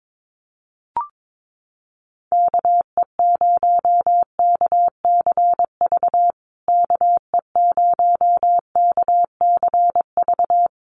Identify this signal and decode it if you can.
Morse code, heard as a keyed tone, about 22 wpm, 700 Hz. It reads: KE0KCV KE0KCV
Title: USA Call Signs Repeated 3x 22wpm